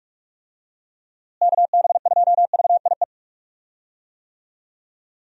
Morse code, heard as a keyed tone, about 45 wpm, 700 Hz. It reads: KB2VIE
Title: USA Call Signs - 6 Characters - Repeated 2x - 45wpm